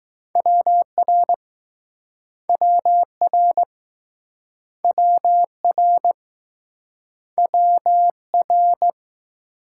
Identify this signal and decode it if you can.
Morse code, heard as a keyed tone, about 19 wpm, 700 Hz. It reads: WR WR WR WR